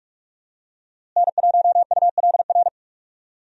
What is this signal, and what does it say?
Morse code, heard as a keyed tone, about 42 wpm, 700 Hz. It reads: N1ULR
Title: USA Call Signs - 5 Characters - Repeated 2x - 45wpm